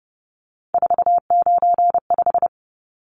Morse code, heard as a keyed tone, about 30 wpm, 700 Hz. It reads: 495
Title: Numbers - 3-Digit 30wpm